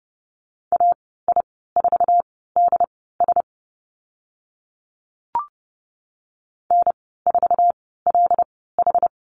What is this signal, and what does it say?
Morse code, heard as a keyed tone, about 30 wpm, spaced as for 17 wpm, 700 Hz. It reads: AI4DS N4LH
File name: Lesson 14g - Number 2 - Call Signs 30wpm